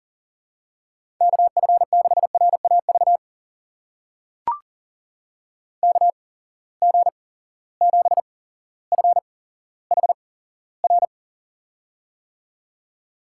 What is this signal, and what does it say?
Morse code, heard as a keyed tone, about 38 wpm, spaced as for 17 wpm, 700 Hz. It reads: KF6RAV KG7FHR
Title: USA Call Signs - 6 Characters - 8x Character Spacing - 40wpm